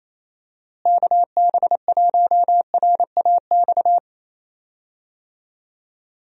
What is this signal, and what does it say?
Morse code, heard as a keyed tone, about 28 wpm, 700 Hz. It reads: KB1RAX
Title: USA Call Signs - 6 Characters - Repeated 2x - 28wpm